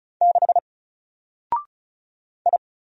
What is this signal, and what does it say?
Morse code, heard as a keyed tone, about 36 wpm, 700 Hz. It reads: 6 I
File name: Single-Letter-Number-35wpm